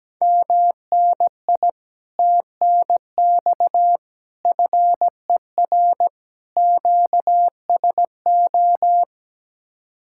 Morse code, heard as a keyed tone, about 17 wpm, 700 Hz. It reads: MNI TNX FER QSO
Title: Speed-Racing Common QSO Phrases Encore - 17wpm